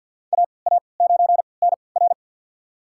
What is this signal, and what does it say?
Morse code, heard as a keyed tone, about 52 wpm, spaced as for 28 wpm, 700 Hz. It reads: AA9NR
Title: USA Call Signs - 5 Characters - 3x Character Spacing - 50wpm